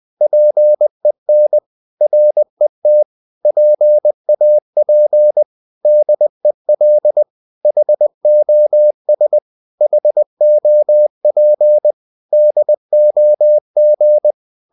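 Morse code, heard as a keyed tone, about 20 wpm, 600 Hz. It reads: PEN RET PAP DEL HOS HOP DOG
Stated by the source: Morse code practice of short Danish words at 20 wpm, from CW med Gnister.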